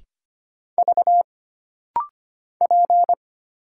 Morse code, heard as a keyed tone, about 25 wpm, 700 Hz. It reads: V P
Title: Single Letter-Number Alt - Rapid-Fire 25wpm